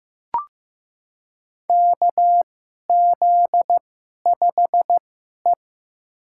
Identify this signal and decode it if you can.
Morse code, heard as a keyed tone, about 15 wpm, spaced as for 11 wpm, 700 Hz. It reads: KZ5E
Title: USA Call Signs - 4 Characters - 2x Character Spacing - 15wpm